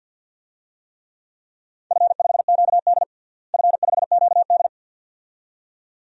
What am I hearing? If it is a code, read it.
Morse code, heard as a keyed tone, about 48 wpm, 700 Hz. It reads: F5QD F5QD
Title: DXCC-100 Call Signs - 4 Characters - Repeated 2x - 50wpm